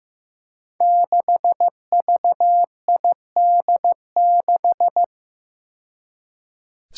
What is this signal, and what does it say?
Morse code, heard as a keyed tone, about 15 wpm, 700 Hz. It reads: 6VID6
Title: Five Challenging Letter-Number Combinations 15wpm